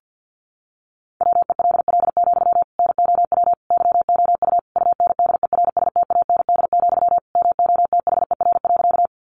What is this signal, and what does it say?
Morse code, heard as a keyed tone, about 48 wpm, 700 Hz. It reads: WELL, NOW YOU UNDERSTAND, MOTHER.
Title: Sentences from Top 600 Words 50wpm